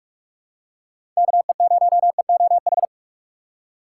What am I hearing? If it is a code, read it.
Morse code, heard as a keyed tone, about 44 wpm, 700 Hz. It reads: KE0EOH
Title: USA Call Signs - 6 Characters - 6x Character Spacing - 45wpm